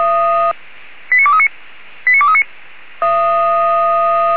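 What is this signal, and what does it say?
Five-tone selective call (ZVEI), heard as quick runs of five tones, tones of about 70 ms. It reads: 89239 89239
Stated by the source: ZVEI five-tone call 89242 received by SDR, channel 464.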